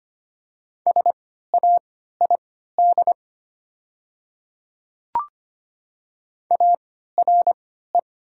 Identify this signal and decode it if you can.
Morse code, heard as a keyed tone, about 25 wpm, spaced as for 14 wpm, 700 Hz. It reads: SAID ARE
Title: 3x Character Spacing - Top 300 Words - 25wpm